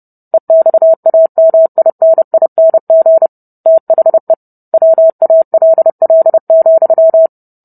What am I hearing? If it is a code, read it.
Morse code, heard as a keyed tone, about 30 wpm, 650 Hz. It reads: EXAMINING THE WALL,